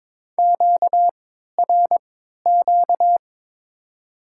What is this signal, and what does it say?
Morse code, heard as a keyed tone, about 22 wpm, spaced as for 13 wpm, 700 Hz. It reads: QRQ
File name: Top 90 QSO Elements - 3x Character Spacing - 22wpm